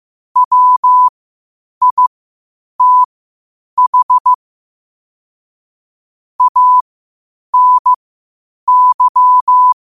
Morse code, heard as a keyed tone, about 15 wpm, spaced as for 8 wpm, 1000 Hz. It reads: WITH ANY
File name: qotdfiles / 8wpm